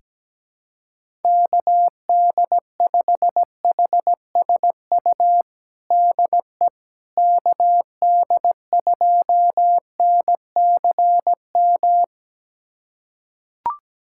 Morse code, heard as a keyed tone, about 17 wpm, 700 Hz. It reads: KD5HSU DE KD2NCM
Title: USA Callsign DE Callsign - 6 Characters - 17wpm